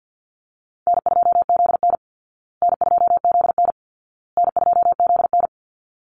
Morse code, heard as a keyed tone, about 50 wpm, 700 Hz. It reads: N2ZN N2ZN N2ZN
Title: Easy Call Signs Repeated 3x - 50wpm